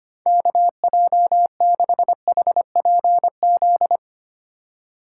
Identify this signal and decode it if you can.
Morse code, heard as a keyed tone, about 25 wpm, 690 Hz. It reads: KJ6HPZ
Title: USA Call Signs - 4x Character Spacing - 25wpm